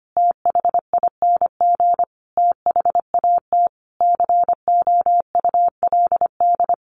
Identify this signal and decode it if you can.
Morse code, heard as a keyed tone, about 25 wpm, 700 Hz. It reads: THING THAT COULD
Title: Sentences from Top 400 Words 25wpm